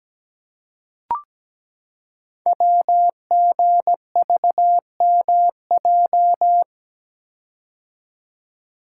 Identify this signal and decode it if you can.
Morse code, heard as a keyed tone, about 17 wpm, 700 Hz. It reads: WGVMJ
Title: Code Groups - 5 Letters - 17wpm